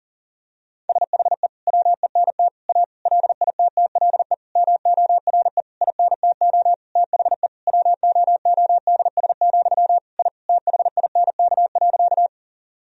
Morse code, heard as a keyed tone, about 37 wpm, 700 Hz. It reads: SHE WENT A LITTLE MORE INTO THE WOODS, I THINK.
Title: Sentences from Top 1000 Words - Review Part 2 - 40wpm